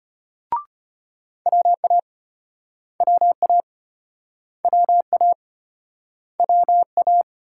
Morse code, heard as a keyed tone, about 35 wpm, 700 Hz. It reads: WA WA WA WA